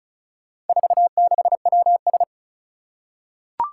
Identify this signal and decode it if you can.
Morse code, heard as a keyed tone, about 36 wpm, 700 Hz. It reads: 46WS
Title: Four Challenging Letter-Number Combinations 35wpm